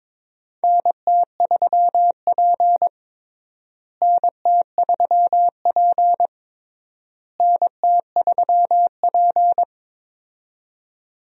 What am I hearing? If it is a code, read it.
Morse code, heard as a keyed tone, about 22 wpm, 700 Hz. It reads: NT3P NT3P NT3P
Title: USA Call Signs - 4 Characters - Repeated 3x - 22wpm